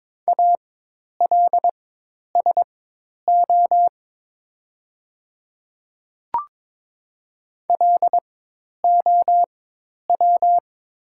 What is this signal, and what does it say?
Morse code, heard as a keyed tone, about 22 wpm, spaced as for 10 wpm, 700 Hz. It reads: ALSO LOW
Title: 4x Character Spacing - Top 300 Words - 22wpm